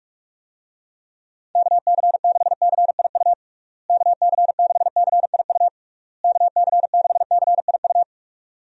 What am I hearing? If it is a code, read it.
Morse code, heard as a keyed tone, about 41 wpm, 700 Hz. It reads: KC6CIU KC6CIU KC6CIU
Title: USA Call Signs - 6 Characters - Repeated 3x - 45wpm